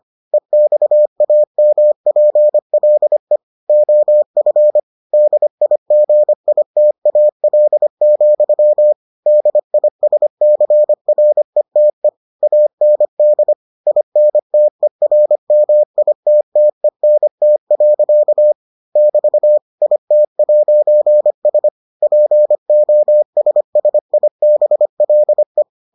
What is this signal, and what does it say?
Morse code, heard as a keyed tone, about 25 wpm, 600 Hz. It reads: EXAMPLE OF DIGITAL, DISCRETE AND INTERMITTENT. <BT> IT'S POSSIBLE